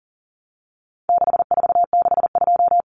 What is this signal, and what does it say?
Morse code, heard as a keyed tone, about 40 wpm, 700 Hz. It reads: 6462
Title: Numbers - 4-Digit 40wpm